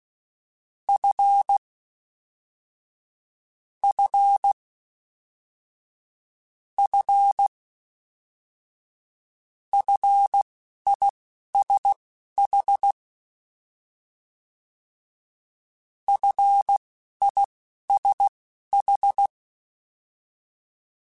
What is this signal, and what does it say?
Morse code, heard as a keyed tone, about 16 wpm, spaced as for 11 wpm, 800 Hz. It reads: F F F FISH FISH
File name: F and L / F and L mp3 / F and L Prefixes 3x2 Twice @10